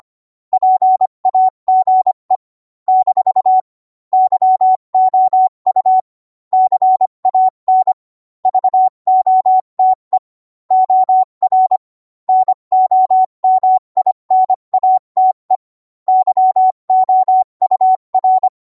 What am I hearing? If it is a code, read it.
Morse code, heard as a keyed tone, about 25 wpm, 750 Hz. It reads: PAGE - YOU CAN VOTE OR NOMINATE YOUR